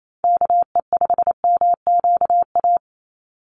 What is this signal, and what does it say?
Morse code, heard as a keyed tone, about 28 wpm, 700 Hz. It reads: KE5MQA
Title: USA Call Signs 28wpm